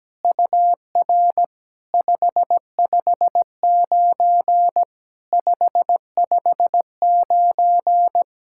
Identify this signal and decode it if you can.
Morse code, heard as a keyed tone, about 17 wpm, 700 Hz. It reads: UR 559 559
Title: SOTA QSO Practice 17wpm